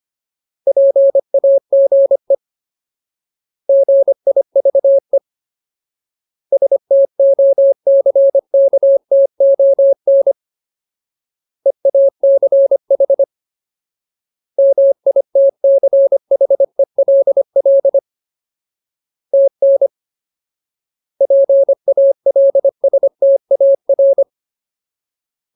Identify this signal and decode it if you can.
Morse code, heard as a keyed tone, about 25 wpm, 550 Hz. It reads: PAGE GIVE STOCKTON EACH MITCHELL TN PALSTAR
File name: Ultimate Word List / 25WPM Ultimate